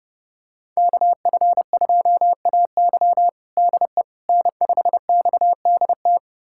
Morse code, heard as a keyed tone, about 30 wpm, 700 Hz. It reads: KF2AY DE N5XDT
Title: USA Callsign DE Callsign - 5 Characters - 30wpm